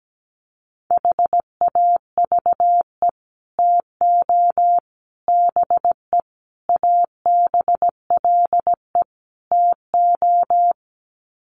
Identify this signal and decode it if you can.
Morse code, heard as a keyed tone, about 17 wpm, 700 Hz. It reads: HAVE TO BE ABLE TO
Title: Sets of 5 Words 17wpm